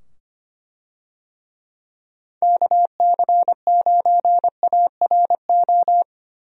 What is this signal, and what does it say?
Morse code, heard as a keyed tone, about 25 wpm, 700 Hz. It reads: KC9ARO
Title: USA Call Signs - 6 Characters - 25wpm